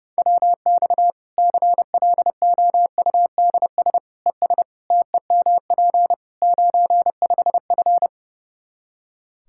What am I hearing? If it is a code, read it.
Morse code, heard as a keyed tone, about 30 wpm, 700 Hz. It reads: WX CLOUDS ES TEMP 95F
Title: Practice QSO - Exchange 2 - Info - 30wpm